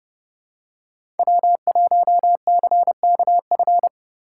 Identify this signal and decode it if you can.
Morse code, heard as a keyed tone, about 29 wpm, 700 Hz. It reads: W1CKF